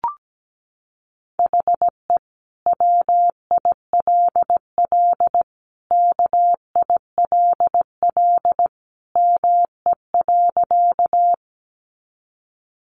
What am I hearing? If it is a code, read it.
Morse code, heard as a keyed tone, about 17 wpm, 700 Hz. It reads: HE WILL KILL ME.
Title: Sentences from Top 1400 Words 17wpm